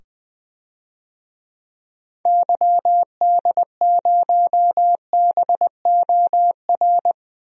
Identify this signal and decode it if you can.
Morse code, heard as a keyed tone, about 20 wpm, 700 Hz. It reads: YD0BOR
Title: DXCC-100 Call Signs - 6 Characters - 20wpm